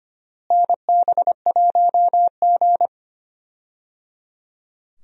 Morse code, heard as a keyed tone, about 25 wpm, 700 Hz. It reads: NB1G
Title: USA Call Signs - 4 Characters - Repeated 3x - 25wpm